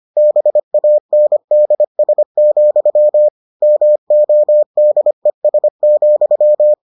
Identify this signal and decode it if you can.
Morse code, heard as a keyed tone, about 25 wpm, 600 Hz. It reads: BANDS, MODES,